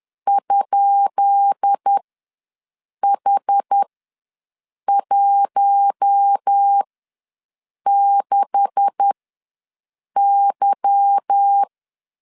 Morse code, heard as a keyed tone, about 10 wpm, spaced as for 6 wpm, 800 Hz. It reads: ?H16Y
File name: LEZIONI / 10 SIGNS / 10B INT ws 10wpm 4134